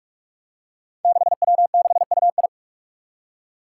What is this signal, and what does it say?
Morse code, heard as a keyed tone, about 44 wpm, 700 Hz. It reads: 6W6UI
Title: Five Challenging Letter-Number Combinations 45wpm